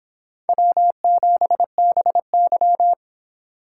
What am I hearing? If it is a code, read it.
Morse code, heard as a keyed tone, about 23 wpm, 700 Hz. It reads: W7BY